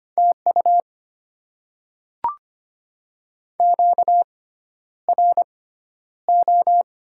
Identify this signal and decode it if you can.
Morse code, heard as a keyed tone, about 25 wpm, spaced as for 11 wpm, 700 Hz. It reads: TU QRO
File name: Top 90 QSO Elements - 6x Character Spacing - 25wpm